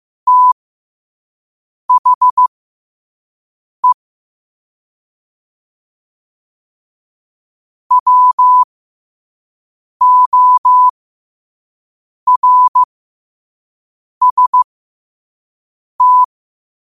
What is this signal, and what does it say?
Morse code, heard as a keyed tone, about 15 wpm, spaced as for 5 wpm, 1000 Hz. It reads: THE WORST
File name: qotdfiles / 5wpm